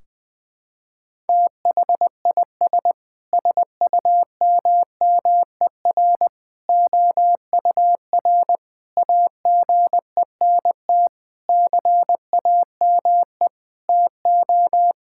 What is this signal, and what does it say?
Morse code, heard as a keyed tone, about 20 wpm, 700 Hz. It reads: THIS SUMMER OUR AGENT CAME TO
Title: Sentences from Top 1600 Words 20wpm